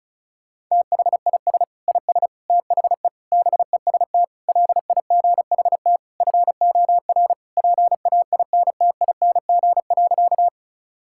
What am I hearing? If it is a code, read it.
Morse code, heard as a keyed tone, about 36 wpm, 700 Hz. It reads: THIS IS THE BEST LIGHT FOR PAINTING.
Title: Sentences from Top 1000 Words - Review Part 6 - 35wpm